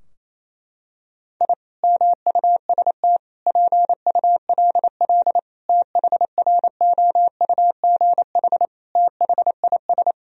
Morse code, heard as a keyed tone, about 28 wpm, 690 Hz. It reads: I MUST PULL THROUGH THIS
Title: Sentences from Top 1000 Words - Review Part 2 - 28wpm